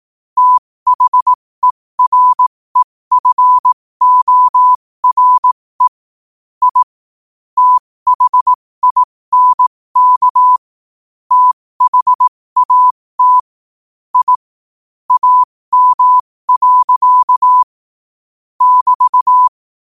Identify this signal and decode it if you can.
Morse code, heard as a keyed tone, about 18 wpm, spaced as for 15 wpm, 1000 Hz. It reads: THEREFORE I THINK THAT I AM. <BT>
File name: qotdfiles / 15wpm